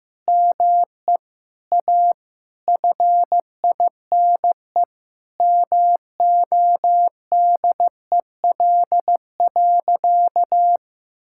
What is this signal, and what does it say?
Morse code, heard as a keyed tone, about 15 wpm, 700 Hz. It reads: ME A FINE MODEL.
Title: Sentences from Top 1000 Words - Review Part 5 - 15wpm